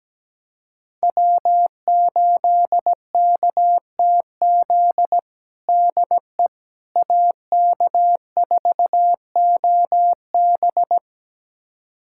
Morse code, heard as a keyed tone, about 17 wpm, 700 Hz. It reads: W8KTZ DE AK4OB
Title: USA Callsign DE Callsign - 5 Characters - 17wpm